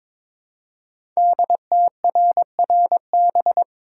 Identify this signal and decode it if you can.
Morse code, heard as a keyed tone, about 22 wpm, 690 Hz. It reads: DTRRB
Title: Code Groups - 5 Letters - 22wpm